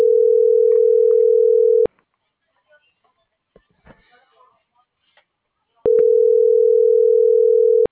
Ringback tone (North America), 440 + 480 Hz.